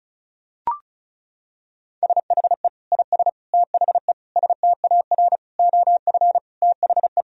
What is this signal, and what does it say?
Morse code, heard as a keyed tone, about 34 wpm, 700 Hz. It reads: SHE IS THE STAR OF THE
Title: Sentences from Top 1000 Words - Review Part 2 - 35wpm